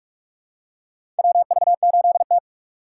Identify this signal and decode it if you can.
Morse code, heard as a keyed tone, about 44 wpm, 700 Hz. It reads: WV8T
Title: USA Call Signs - 4 Characters - 6x Character Spacing - 45wpm